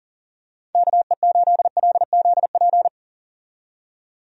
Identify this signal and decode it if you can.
Morse code, heard as a keyed tone, about 39 wpm, 700 Hz. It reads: KE8LZP
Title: USA Call Signs - 6 Characters - 40wpm